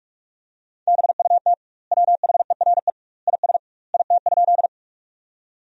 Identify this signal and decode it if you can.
Morse code, heard as a keyed tone, about 46 wpm, 700 Hz. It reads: BUT WHERE IS IT?
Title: 2x Word-Spacing - Sentences from Top 200 Words - 45wpm